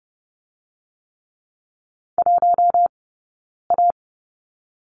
Morse code, heard as a keyed tone, about 30 wpm, 700 Hz. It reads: 1 A
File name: course / 30:10wpm / 099 Lesson 17d - Letter u - Two Characters 30wpm